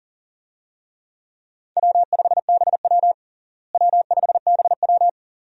Morse code, heard as a keyed tone, about 40 wpm, 700 Hz. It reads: W5BW W5BW
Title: USA Call Signs - 4 Characters - Repeated 2x - 40wpm